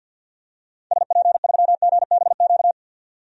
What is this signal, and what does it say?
Morse code, heard as a keyed tone, about 48 wpm, 700 Hz. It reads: SP3GBQ
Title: DXCC-100 Call Signs - 6 Characters - 50wpm